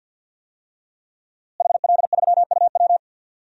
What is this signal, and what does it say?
Morse code, heard as a keyed tone, about 53 wpm, 700 Hz. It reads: HL3UW
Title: DXCC-100 Call Signs - 5 Characters - Repeated 2x - 50wpm